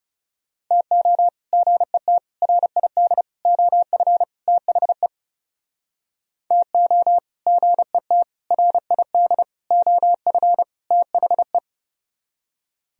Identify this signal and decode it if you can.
Morse code, heard as a keyed tone, about 38 wpm, 700 Hz. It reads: TO GET RID OF THE TO GET RID OF THE